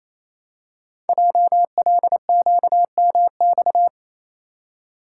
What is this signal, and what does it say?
Morse code, heard as a keyed tone, about 27 wpm, 700 Hz. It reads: JLQMX
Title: Code Groups - 5 Letters - 28wpm